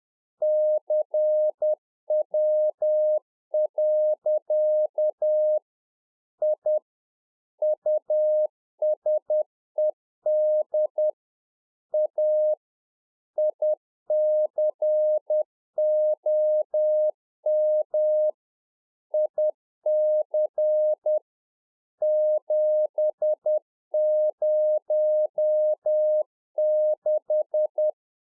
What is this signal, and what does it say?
Morse code, heard as a keyed tone, about 10 wpm, 610 Hz. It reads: CW. I USED A ICOM IC 706
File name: 2011-04-mp3 / UNFOGETTABLE QSO 10WPM